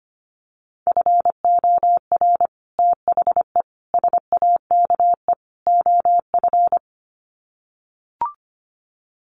Morse code, heard as a keyed tone, about 25 wpm, 700 Hz. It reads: FOR THE SAKE OF
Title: Sets of 4 Words 25wpm